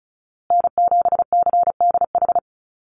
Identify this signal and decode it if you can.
Morse code, heard as a keyed tone, about 34 wpm, 700 Hz. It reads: N7CDH
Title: USA Call Signs Repeated 3x 35wpm